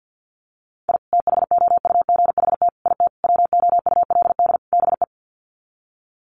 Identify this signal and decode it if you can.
Morse code, heard as a keyed tone, about 50 wpm, 700 Hz. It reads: I THOUGHT IT WOULD BE
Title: Sets of 5 Words Encore 50wpm